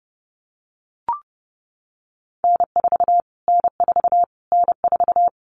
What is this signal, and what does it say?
Morse code, heard as a keyed tone, about 30 wpm, 700 Hz. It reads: N4 N4 N4
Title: Callsign Prefixes for Top 25 DXCC - Repeated 3x - 30wpm